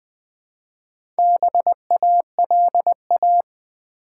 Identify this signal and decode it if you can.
Morse code, heard as a keyed tone, about 20 wpm, 700 Hz. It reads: BALA
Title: Build A Word - 2 Letters At A Time - 8-Letter Words - 20wpm